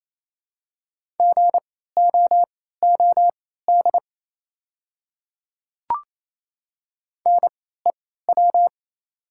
Morse code, heard as a keyed tone, about 28 wpm, spaced as for 16 wpm, 700 Hz. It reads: GOOD NEW
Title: 3x Character Spacing - Top 200 Words - 28wpm